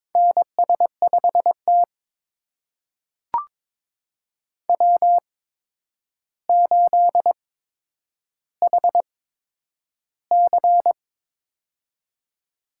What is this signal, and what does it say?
Morse code, heard as a keyed tone, about 22 wpm, spaced as for 9 wpm, 700 Hz. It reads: NS5T W8HC
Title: USA Call Signs - 4 Characters - 8x Character Spacing - 22wpm